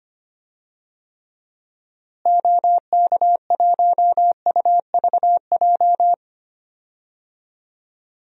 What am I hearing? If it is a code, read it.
Morse code, heard as a keyed tone, about 25 wpm, 700 Hz. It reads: OK1UVJ
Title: DXCC-100 Call Signs - 6 Characters - Repeated 2x - 25wpm